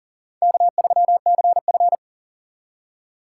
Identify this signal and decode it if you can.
Morse code, heard as a keyed tone, about 40 wpm, 700 Hz. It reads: K3CF
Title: USA Call Signs - 4 Characters - 2x Character Spacing - 40wpm